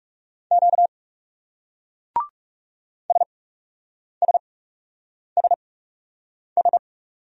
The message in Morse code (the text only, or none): Q SSSS